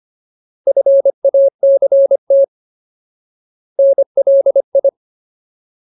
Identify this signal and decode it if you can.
Morse code, heard as a keyed tone, about 25 wpm, 550 Hz. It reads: FACT NLI
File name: Ultimate Word List / 25WPM Ultimate